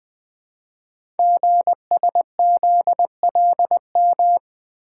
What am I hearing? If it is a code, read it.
Morse code, heard as a keyed tone, about 20 wpm, 700 Hz. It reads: GSZLM
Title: Code Groups - 5 Letters - 20wpm